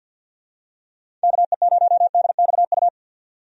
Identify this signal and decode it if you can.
Morse code, heard as a keyed tone, about 51 wpm, 700 Hz. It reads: KE0DXU
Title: USA Call Signs - 6 Characters - 50wpm